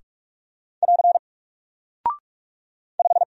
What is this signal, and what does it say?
Morse code, heard as a keyed tone, about 44 wpm, 700 Hz. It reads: <AR> 5
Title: Single Character (Letter:Number:Punctuation:Prosign) Alt 45wpm